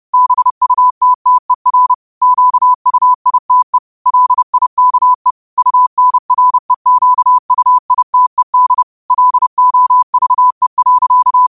Morse code, heard as a keyed tone, about 30 wpm, 1000 Hz. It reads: BUTTER QUITE LIKE UNREQUITED LOVE.